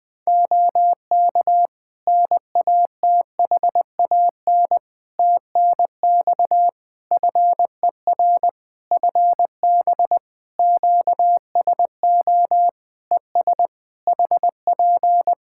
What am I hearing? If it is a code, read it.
Morse code, heard as a keyed tone, about 20 wpm, 700 Hz. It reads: OK NATHAN TNX FER FB QSO ES HP
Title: Practice QSO - Exchange 4 - Ending - 20wpm